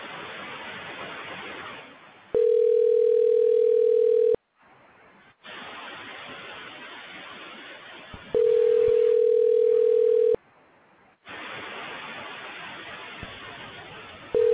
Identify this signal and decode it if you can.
Ringback tone (North America), 440 + 480 Hz.